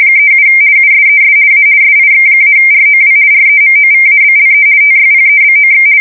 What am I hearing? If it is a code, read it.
RTTY, about 50 baud. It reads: 37 USES A SIX-ROW REMOVABLE TYPEBOX
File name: fsk170-50B / teletype-corp